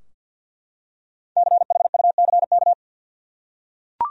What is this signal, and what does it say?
Morse code, heard as a keyed tone, about 52 wpm, 700 Hz. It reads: CHUCK
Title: USA Common Names - Max 5 Letters - 50wpm